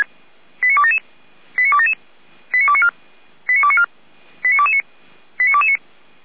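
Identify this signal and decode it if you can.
Five-tone selective call (ZVEI), heard as quick runs of five tones, tones of about 70 ms. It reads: 89277 89277 89294 89294 89229 89229
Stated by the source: ZVEI five-tone selective call received by SDR.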